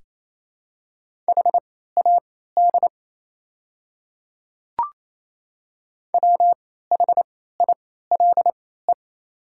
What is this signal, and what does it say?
Morse code, heard as a keyed tone, about 28 wpm, spaced as for 16 wpm, 700 Hz. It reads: HAD WHILE
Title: 3x Character Spacing - Top 300 Words - 28wpm